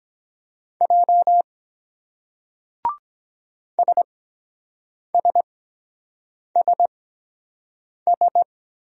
Morse code, heard as a keyed tone, about 23 wpm, spaced as for 7 wpm, 700 Hz. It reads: J SSSS